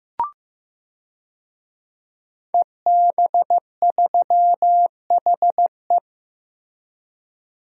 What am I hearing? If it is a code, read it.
Morse code, heard as a keyed tone, about 15 wpm, 700 Hz. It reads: EB3HE
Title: DXCC-100 Call Signs - 5 Characters - 15wpm